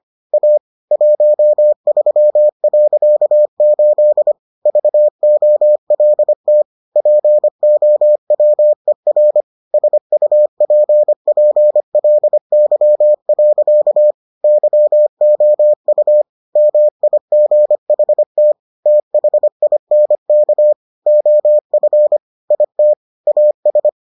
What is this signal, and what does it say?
Morse code, heard as a keyed tone, about 25 wpm, 600 Hz. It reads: A 13.8 VOLT POWER SUPPLY. YOU MIGHT THINK OF IT AS